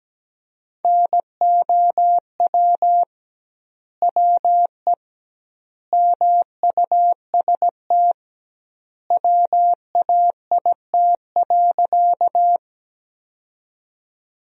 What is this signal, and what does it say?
Morse code, heard as a keyed tone, about 17 wpm, 700 Hz. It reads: NOW WE MUST WAIT.
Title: 2x Word-Spacing - Sentences from Top 500 Words - 17wpm